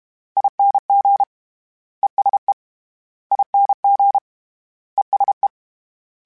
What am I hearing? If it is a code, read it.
Morse code, heard as a keyed tone, about 32 wpm, 800 Hz. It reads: ING ESE ING ESE
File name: Short Runs / Short Runs-mp3s / ESE ING @30